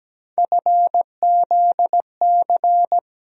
Morse code, heard as a keyed tone, about 17 wpm, 700 Hz. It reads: FZC
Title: USA Call Sign Postfixes - 3 Letters 17wpm